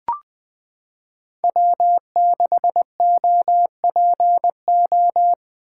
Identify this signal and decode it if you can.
Morse code, heard as a keyed tone, about 20 wpm, 700 Hz. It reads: W6OPO
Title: K1USN SST QSO Practice - 20wpm